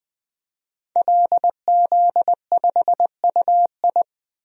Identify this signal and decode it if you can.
Morse code, heard as a keyed tone, about 20 wpm, 700 Hz. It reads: LZ5UI
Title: DXCC-100 Call Signs - 5 Characters - 20wpm